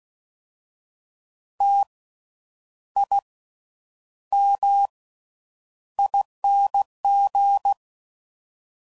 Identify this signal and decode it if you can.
Morse code, heard as a keyed tone, about 16 wpm, spaced as for 8 wpm, 800 Hz. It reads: TIMING
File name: Suffixes / Special INGs / Slow With Faster Ings